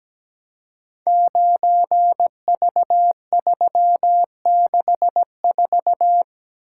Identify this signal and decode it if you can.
Morse code, heard as a keyed tone, about 17 wpm, 700 Hz. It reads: 9V364
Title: Code Groups - 5 Letter:Numbers - 17wpm